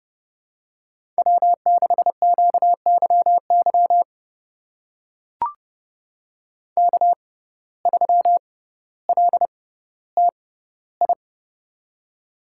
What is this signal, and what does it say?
Morse code, heard as a keyed tone, about 30 wpm, spaced as for 15 wpm, 700 Hz. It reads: W6QYY K3LTI
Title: USA Call Signs - 5 Characters - 6x Character Spacing - 30wpm